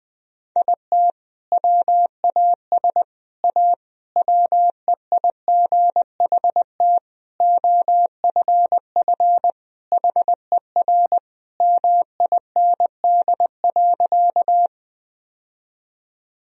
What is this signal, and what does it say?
Morse code, heard as a keyed tone, about 20 wpm, 700 Hz. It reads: IT WAS A WEIGHT OFF HER MIND.
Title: Sentences from Top 1000 Words - Review Part 3 - 20wpm